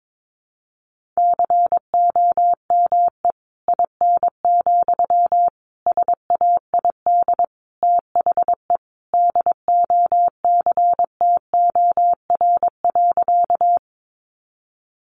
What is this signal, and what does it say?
Morse code, heard as a keyed tone, about 22 wpm, 700 Hz. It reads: COME IN, SAID THE DOCTOR.
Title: Sentences from Top 1500 Words 22wpm